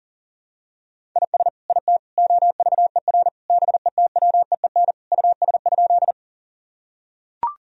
Morse code, heard as a keyed tone, about 40 wpm, 700 Hz. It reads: IS IT OVER BETWEEN US?
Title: Sentences from Top 1000 Words - Review Part 1 - 40wpm